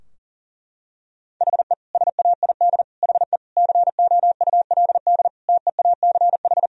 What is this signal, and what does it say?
Morse code, heard as a keyed tone, about 40 wpm, 700 Hz. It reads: HE SAID HE COULD TEACH
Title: Sentences from Top 1000 Words - Review Part 2 - 40wpm